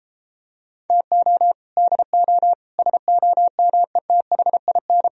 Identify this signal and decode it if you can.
Morse code, heard as a keyed tone, about 33 wpm, 700 Hz. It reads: TO DO SOMETHIN